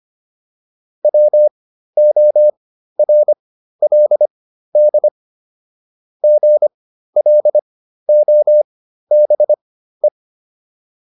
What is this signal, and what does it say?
Morse code, heard as a keyed tone, about 25 wpm, spaced as for 13 wpm, 600 Hz. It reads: WORLD GLOBE
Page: intermediate-practice-files